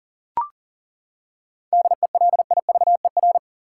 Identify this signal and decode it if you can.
Morse code, heard as a keyed tone, about 40 wpm, 700 Hz. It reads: DELIVER
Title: Seven Letter Words - 40wpm